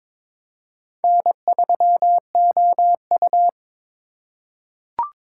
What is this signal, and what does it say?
Morse code, heard as a keyed tone, about 22 wpm, 700 Hz. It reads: N3OU
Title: USA Call Signs - 2x Character Spacing - 22wpm